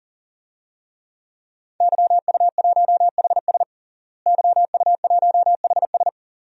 Morse code, heard as a keyed tone, about 37 wpm, 700 Hz. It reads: YU1HS YU1HS
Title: DXCC-100 Call Signs - 5 Characters - Repeated 2x - 40wpm